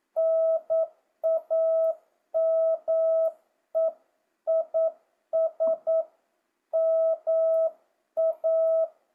Morse code, heard as a keyed tone, about 9 wpm, 650 Hz. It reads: NAME IS MA
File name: jh8cbh / 2025cw